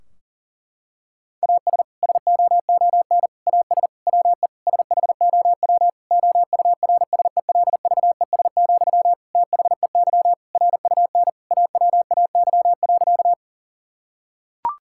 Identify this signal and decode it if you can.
Morse code, heard as a keyed tone, about 40 wpm, 700 Hz. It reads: AS SOON AS WE SHOW OURSELVES, THEY RUN AWAY.
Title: Sentences from Top 1600 Words 40wpm